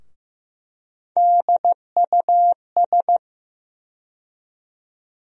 Morse code, heard as a keyed tone, about 15 wpm, 700 Hz. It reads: DUS
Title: USA Call Sign Postfixes - 3 Letters 15wpm